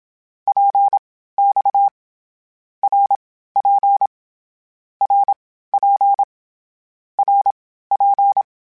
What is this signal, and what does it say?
Morse code, heard as a keyed tone, about 27 wpm, spaced as for 15 wpm, 800 Hz. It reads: PX RP RP RP
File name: Short Runs / Short Runs-mp3s / Balanced @25